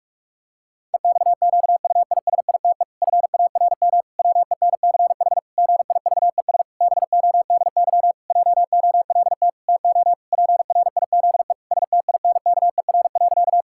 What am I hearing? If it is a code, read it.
Morse code, heard as a keyed tone, about 46 wpm, 700 Hz. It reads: EXQUISITE FARM WENCH GIVES BODY JOLT TO PRIZE STINKER.
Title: Pangram Challenge v2 - 45wpm